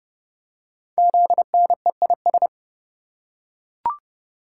Morse code, heard as a keyed tone, about 29 wpm, 700 Hz. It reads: ZNEIS
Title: Code Groups - 5 Letters - 30wpm